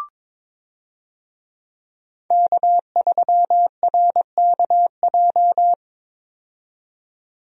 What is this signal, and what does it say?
Morse code, heard as a keyed tone, about 22 wpm, 700 Hz. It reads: K3RKJ
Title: USA Call Signs - 5 Characters - 22wpm